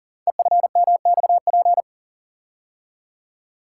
Morse code, heard as a keyed tone, about 38 wpm, 690 Hz. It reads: EFMXP
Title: Code Groups - 5 Letters - 40wpm